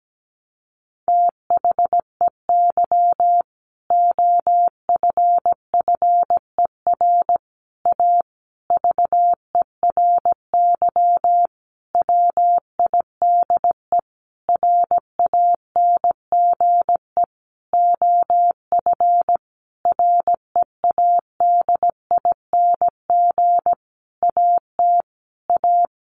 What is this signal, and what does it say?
Morse code, heard as a keyed tone, about 17 wpm, 700 Hz. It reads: THEY OFFER A VERY WIDE RANGE OF READING AT A
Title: Sentences from Top 900 Words 17wpm